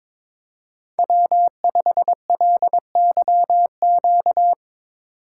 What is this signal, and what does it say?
Morse code, heard as a keyed tone, about 23 wpm, 700 Hz. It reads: W5LYQ
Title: Speed Racing - USA Call Signs 22wpm